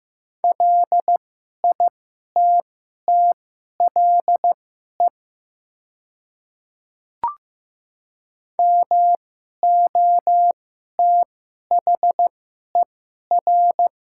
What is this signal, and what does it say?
Morse code, heard as a keyed tone, about 15 wpm, spaced as for 11 wpm, 700 Hz. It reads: LITTLE MOTHER
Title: 2x Character Spacing - Top 300 Words - 15wpm